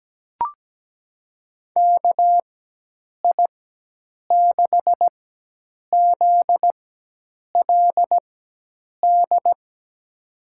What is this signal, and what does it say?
Morse code, heard as a keyed tone, about 17 wpm, spaced as for 8 wpm, 700 Hz. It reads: KI6ZLD
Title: USA Call Signs - 4x Character Spacing - 17wpm